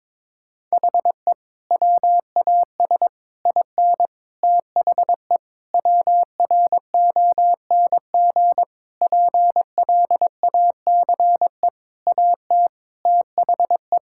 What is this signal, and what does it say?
Morse code, heard as a keyed tone, about 23 wpm, 700 Hz. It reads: HE WAS IN THE WRONG PLACE AT THE
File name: practice / 22wpm / Sentences with 8-Word Phrases - 22wpm